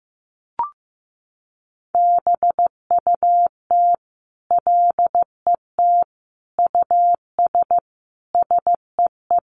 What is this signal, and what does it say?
Morse code, heard as a keyed tone, about 15 wpm, 700 Hz. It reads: BUT LET US SEE
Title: Sentences from Top 500 Words 15wpm